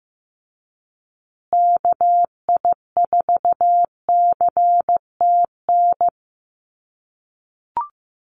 Morse code, heard as a keyed tone, about 15 wpm, 700 Hz. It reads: KI4CTN
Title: USA Call Signs Repeated 2x 15wpm